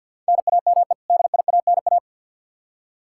Morse code, heard as a keyed tone, about 52 wpm, 700 Hz. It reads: NAME DIANA
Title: Practice QSO - Exchange 1 - Beginning - 50wpm